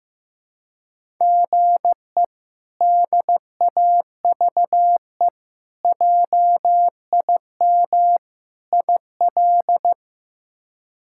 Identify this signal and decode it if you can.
Morse code, heard as a keyed tone, about 15 wpm, 700 Hz. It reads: GE DAVE JIM IL
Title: K1USN SST QSO Practice - 15wpm